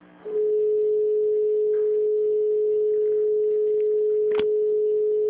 Dial tone (Europe, 425 Hz).